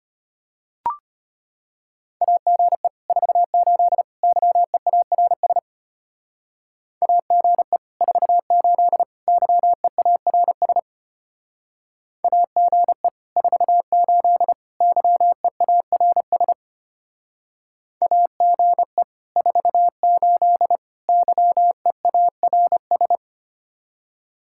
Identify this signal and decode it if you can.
Morse code, heard as a keyed tone, about 30 wpm, 700 Hz. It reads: AGE 48 YEARS AGE 48 YEARS AGE 48 YEARS AGE 48 YEARS
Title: Speed-Racing Common QSO Phrases - 25wpm